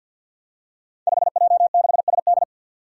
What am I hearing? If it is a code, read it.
Morse code, heard as a keyed tone, about 52 wpm, 700 Hz. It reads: 5J6SD
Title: Five Challenging Letter-Number Combinations 50wpm